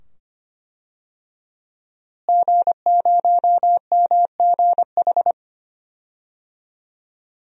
Morse code, heard as a keyed tone, about 25 wpm, 700 Hz. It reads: G0MGH
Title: DXCC-100 Call Signs - 5 Characters - Repeated 2x - 25wpm